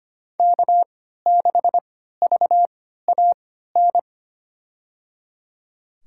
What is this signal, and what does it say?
Morse code, heard as a keyed tone, about 25 wpm, spaced as for 14 wpm, 700 Hz. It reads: K6VAN
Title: USA Call Signs - 5 Characters - 3x Character Spacing - 25wpm